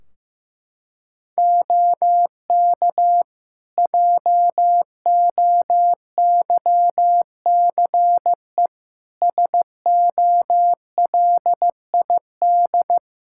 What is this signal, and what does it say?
Morse code, heard as a keyed tone, about 15 wpm, 700 Hz. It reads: OK JOYCE SOLID
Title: Practice QSO - Exchange 3 - More Info - 15wpm